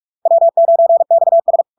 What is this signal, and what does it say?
Morse code, heard as a keyed tone, about 46 wpm, 670 Hz. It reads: W9XS
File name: LARK-FD-2022-Log-in-CW-45wpm-670-Hz